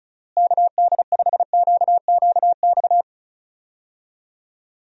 Morse code, heard as a keyed tone, about 36 wpm, 700 Hz. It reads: KD5QQX
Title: USA Call Signs - 6 Characters - Repeated 2x - 35wpm